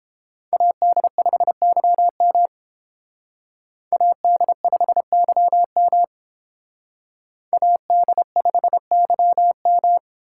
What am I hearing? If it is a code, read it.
Morse code, heard as a keyed tone, about 31 wpm, 690 Hz. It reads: AD5YM AD5YM AD5YM